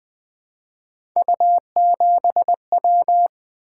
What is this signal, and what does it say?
Morse code, heard as a keyed tone, about 20 wpm, 700 Hz. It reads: U7W
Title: Three Letter-Number Combinations - 20wpm